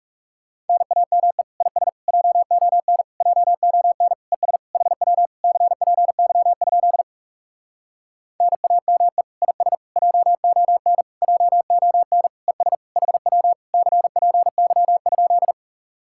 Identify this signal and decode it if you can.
Morse code, heard as a keyed tone, about 40 wpm, 700 Hz. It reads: NAME IS JON JON ES HW CPY? NAME IS JON JON ES HW CPY?